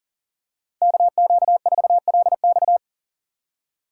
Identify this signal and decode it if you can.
Morse code, heard as a keyed tone, about 40 wpm, 700 Hz. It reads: KQ4LX
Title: USA Call Signs - 5 Characters - 2x Character Spacing - 40wpm